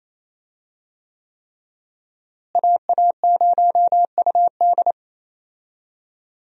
Morse code, heard as a keyed tone, about 27 wpm, 700 Hz. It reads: AA0UD
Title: USA Call Signs - 5 Characters - Repeated 3x - 28wpm